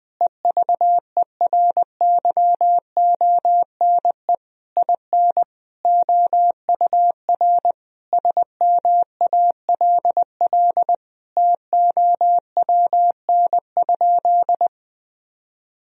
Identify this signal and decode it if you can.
Morse code, heard as a keyed tone, about 21 wpm, 700 Hz. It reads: EVERYONE IN OUR SMALL TOWN?
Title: Sentences from Top 1000 Words - Review Part 4 - 20wpm